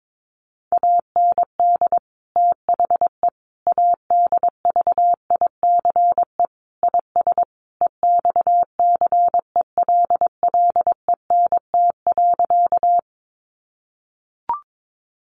Morse code, heard as a keyed tone, about 22 wpm, 700 Hz. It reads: AND THE ADVICE IS EXCELLENT.
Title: Sentences from Top 1100 Words 22wpm